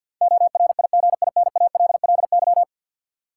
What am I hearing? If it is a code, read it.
Morse code, heard as a keyed tone, about 53 wpm, 700 Hz. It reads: ORIGINALLY
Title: Sentences from Top 1500 Words - New Word Drill Alt - 50wpm